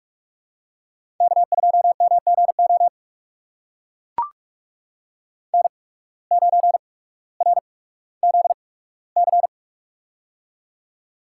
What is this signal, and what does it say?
Morse code, heard as a keyed tone, about 44 wpm, spaced as for 19 wpm, 700 Hz. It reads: K2MGO N9RZC
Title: USA Call Signs - 5 Characters - 8x Character Spacing - 45wpm